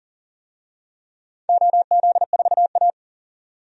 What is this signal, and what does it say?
Morse code, heard as a keyed tone, about 40 wpm, 700 Hz. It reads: OZ4A
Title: DXCC-100 Call Signs - 4 Characters - 40wpm